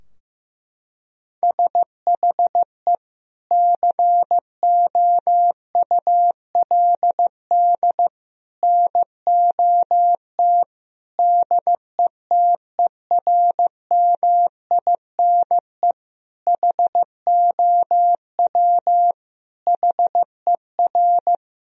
Morse code, heard as a keyed tone, about 15 wpm, 700 Hz. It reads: SHE COULD NOT DETERMINE HOW HER
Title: Sentences from Top 1000 Words - Review Part 4 - 15wpm